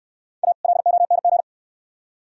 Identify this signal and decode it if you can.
Morse code, heard as a keyed tone, about 67 wpm, 700 Hz. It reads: A LONG